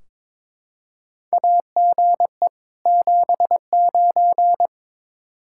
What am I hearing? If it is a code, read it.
Morse code, heard as a keyed tone, about 22 wpm, 700 Hz. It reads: AGE 79
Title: K1USN SST QSO Practice 22wpm